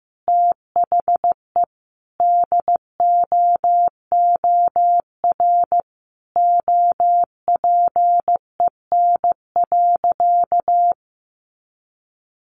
Morse code, heard as a keyed tone, about 14 wpm, 700 Hz. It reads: THE DOOR OPEN.
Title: Sentences from Top 1000 Words 15wpm